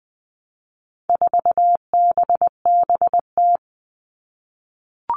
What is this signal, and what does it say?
Morse code, heard as a keyed tone, about 20 wpm, 700 Hz. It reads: 4BBT
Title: All Cut Numbers - 4-Digit 20wpm